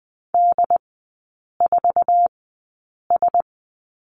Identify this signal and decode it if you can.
Morse code, heard as a keyed tone, about 20 wpm, spaced as for 8 wpm, 700 Hz. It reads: D4S
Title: 093 Lesson 16e - Letter c - Three Characters 20wpm